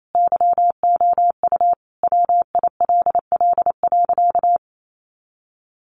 Morse code, heard as a keyed tone, about 28 wpm, 700 Hz. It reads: YOU WILL.
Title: Sentences from Top 1400 Words 28wpm